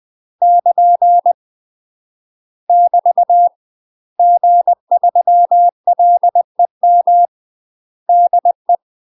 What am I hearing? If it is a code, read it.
Morse code, heard as a keyed tone, about 20 wpm, 700 Hz. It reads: <KN> <BT> G3LEM DE
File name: UK MASTER QSOs TRAINING / 20WPM / UK MASTER QSOs 0032